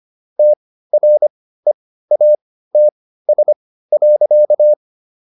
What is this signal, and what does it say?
Morse code, heard as a keyed tone, about 25 wpm, spaced as for 15 wpm, 600 Hz. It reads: TREATS.